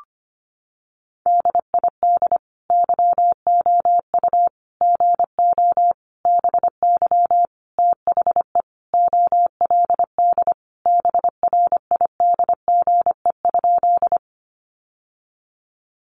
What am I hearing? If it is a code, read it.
Morse code, heard as a keyed tone, about 25 wpm, 700 Hz. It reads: DID YOU GO BY THE OLD BRIDGE?
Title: Sentences from Top 1200 Words 25wpm